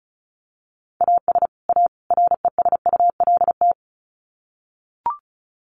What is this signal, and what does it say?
Morse code, heard as a keyed tone, about 34 wpm, 700 Hz. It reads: AS A RESULT
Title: Sets of 3 Words 35wpm